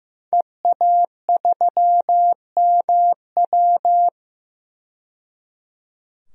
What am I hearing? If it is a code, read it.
Morse code, heard as a keyed tone, about 15 wpm, 700 Hz. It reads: EA3MW
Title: DXCC-100 Call Signs - 5 Characters - 15wpm